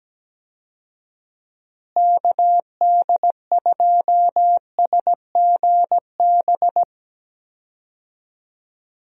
Morse code, heard as a keyed tone, about 17 wpm, 700 Hz. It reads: KD2SGB
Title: USA Call Signs - 6 Characters - Repeated 2x - 17wpm